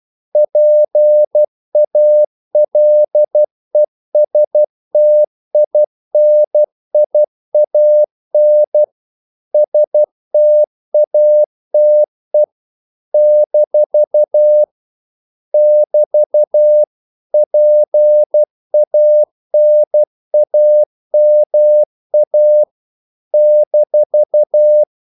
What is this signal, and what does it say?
Morse code, heard as a keyed tone, about 12 wpm, 600 Hz. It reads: PALESTINIAN STATE - <BT> PANAMA -
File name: CW / Practice / NationalCapitals12